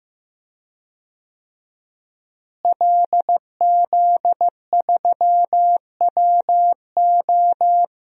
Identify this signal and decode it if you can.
Morse code, heard as a keyed tone, about 15 wpm, 700 Hz. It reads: LZ3WO
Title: DXCC-100 Call Signs - 5 Characters - Repeated 2x - 15wpm